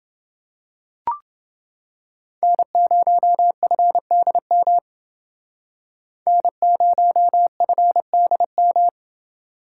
Morse code, heard as a keyed tone, about 28 wpm, 700 Hz. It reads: N0FDM N0FDM